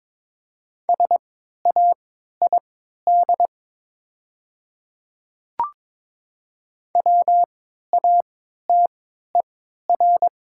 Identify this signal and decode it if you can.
Morse code, heard as a keyed tone, about 22 wpm, spaced as for 12 wpm, 700 Hz. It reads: SAID WATER
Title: 3x Character Spacing - Top 100 Words - 22wpm